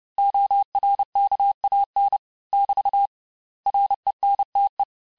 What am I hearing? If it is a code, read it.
Morse code, heard as a keyed tone, about 30 wpm, 800 Hz. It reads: ORKAN <BT> RENTE
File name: cours-du-mercredi / audios / cours-854-mots-vitesse-28-mpm